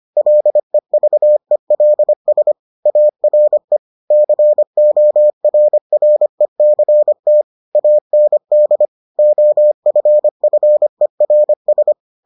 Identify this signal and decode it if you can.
Morse code, heard as a keyed tone, about 25 wpm, 600 Hz. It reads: LEVELS ARE CORRECT AND OFFERS